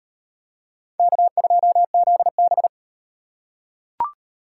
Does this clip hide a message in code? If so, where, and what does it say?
Yes: Morse code from 1.0 to 2.7 s, heard as a keyed tone, about 35 wpm, 700 Hz. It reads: K2ZB